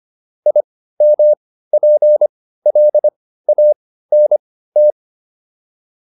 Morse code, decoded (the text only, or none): IMPLANT